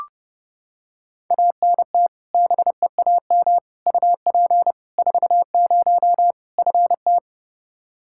Morse code, heard as a keyed tone, about 30 wpm, 700 Hz. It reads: ANT BEAM UP 40 FT
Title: Practice QSO - Exchange 2 - Info - 30wpm